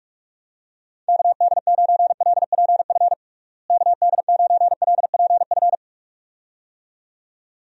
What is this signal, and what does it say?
Morse code, heard as a keyed tone, about 46 wpm, 700 Hz. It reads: KD9LPF KD9LPF
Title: USA Call Signs - 6 Characters - Repeated 2x - 45wpm